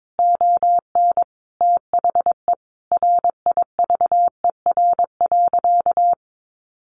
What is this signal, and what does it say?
Morse code, heard as a keyed tone, about 22 wpm, 700 Hz. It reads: ON THE RIVER.
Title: Sentences from Top 1300 Words 22wpm